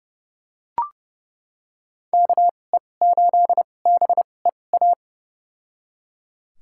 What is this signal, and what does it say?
Morse code, heard as a keyed tone, about 30 wpm, spaced as for 22 wpm, 700 Hz. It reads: KE8BEA
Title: USA Call Signs - 6 Characters - 2x Character Spacing - 30wpm